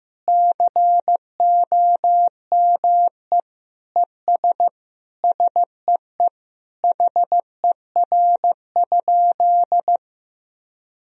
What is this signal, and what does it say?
Morse code, heard as a keyed tone, about 15 wpm, 700 Hz. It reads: COME ES SEE HER?
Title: Sentences from Top 100 Words and Abbreviations 15wpm